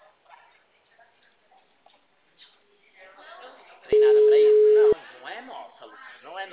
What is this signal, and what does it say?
Ringback tone (Europe, 425 Hz), with speech.
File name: out- -2015- - - .42030